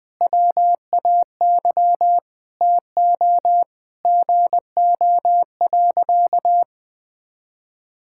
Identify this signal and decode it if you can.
Morse code, heard as a keyed tone, about 19 wpm, 700 Hz. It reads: WAY TO GO.
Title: Sentences with 6-Word Phrases - Spoken Beginning - 20wpm